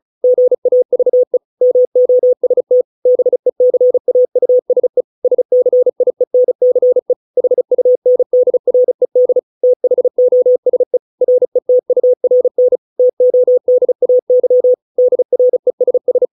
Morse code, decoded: GAVE MOST BECAUSE SCIENCE HUNDRED THOSE RETURN TODAY DRESS

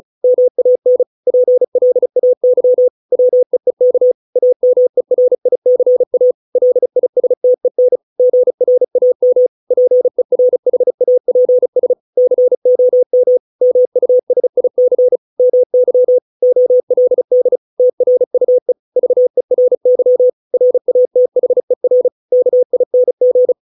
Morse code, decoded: MAN PLAY WEEK AMERICA LISTEN GRAM PERHAPS COM MUSIC MY OLD TRUE VERY RATHER KING

